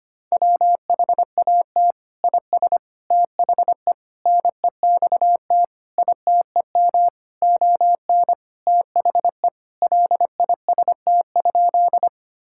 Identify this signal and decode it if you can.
Morse code, heard as a keyed tone, about 25 wpm, 700 Hz. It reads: WHAT IS THE NEXT ITEM ON THE LIST?
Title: Sentences from Top 1000 Words - Review Part 6 - 25wpm